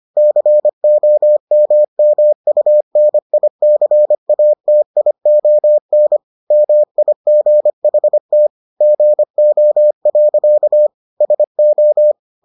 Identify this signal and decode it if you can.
Morse code, heard as a keyed tone, about 25 wpm, 600 Hz. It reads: COMMUNICATION MIGHT GO. SO